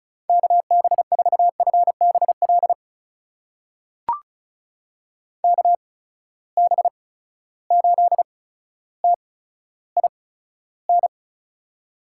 Morse code, heard as a keyed tone, about 36 wpm, spaced as for 15 wpm, 700 Hz. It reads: KB4FBL KB8TIN
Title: USA Call Signs - 6 Characters - 8x Character Spacing - 35wpm